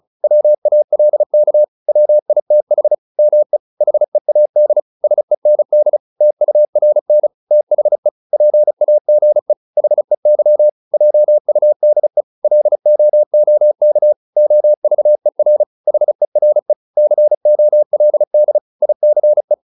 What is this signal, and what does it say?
Morse code, heard as a keyed tone, about 34 wpm, 600 Hz. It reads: WALK WITH ME HEAD SEND TURN THE PAGE HEY JUDE LOOK OVER HERE COLD ICE